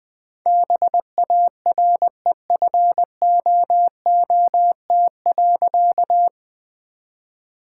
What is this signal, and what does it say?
Morse code, heard as a keyed tone, about 20 wpm, 700 Hz. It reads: BAREFOOT.
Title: All-Age Appropriate Jokes - 20wpm